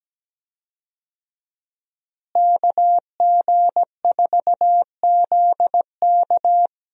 Morse code, heard as a keyed tone, about 17 wpm, 700 Hz. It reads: KG4ZK
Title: USA Call Signs - 5 Characters - Repeated 3x - 17wpm